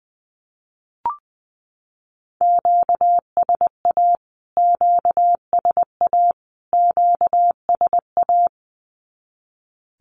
Morse code, heard as a keyed tone, about 20 wpm, 700 Hz. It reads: QSA QSA QSA
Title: Q-Codes Repeated 3x 20wpm